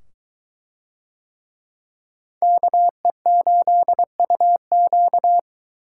Morse code, heard as a keyed tone, about 22 wpm, 700 Hz. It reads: KE8UQ